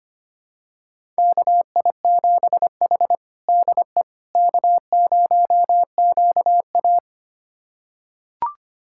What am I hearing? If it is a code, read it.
Morse code, heard as a keyed tone, about 25 wpm, 700 Hz. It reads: KI7H DE K0QA
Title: USA Callsign DE Callsign - 4 Characters - 25wpm